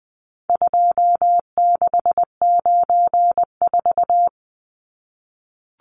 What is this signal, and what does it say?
Morse code, heard as a keyed tone, about 20 wpm, 700 Hz. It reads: 2694
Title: Numbers - 4-Digit 20wpm